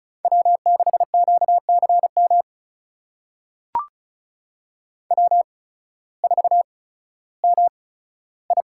Morse code, heard as a keyed tone, about 36 wpm, spaced as for 16 wpm, 700 Hz. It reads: W6QCM W4MI
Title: USA Call Signs - 8x Character Spacing - 35wpm